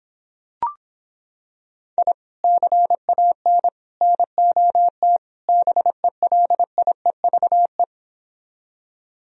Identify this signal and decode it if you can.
Morse code, heard as a keyed tone, about 26 wpm, 700 Hz. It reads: I CAN NOT BELIEVE